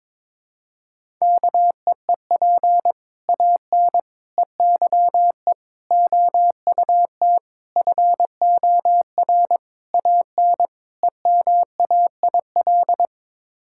Morse code, heard as a keyed tone, about 22 wpm, 700 Hz. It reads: KEEP AN EYE OUT FOR AN EMAIL
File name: Sets of 7 Words - 22wpm